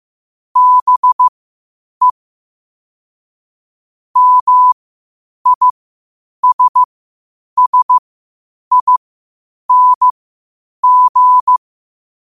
Morse code, heard as a keyed tone, about 15 wpm, spaced as for 8 wpm, 1000 Hz. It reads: BE MISSING
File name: qotdfiles / 8wpm